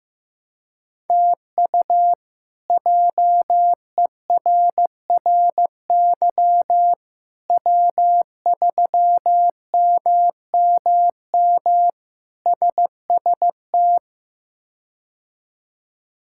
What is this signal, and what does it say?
Morse code, heard as a keyed tone, about 15 wpm, 700 Hz. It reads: TU JERRY W3MMM SST
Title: K1USN SST QSO Practice - 15wpm